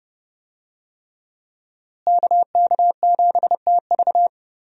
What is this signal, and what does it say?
Morse code, heard as a keyed tone, about 30 wpm, 700 Hz. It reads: KK7TV
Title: USA Call Signs - 5 Characters - Repeated 3x - 30wpm